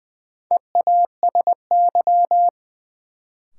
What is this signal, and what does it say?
Morse code, heard as a keyed tone, about 20 wpm, 700 Hz. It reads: EASY
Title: Sentences from Top 700 Words - New Word Drill - 20wpm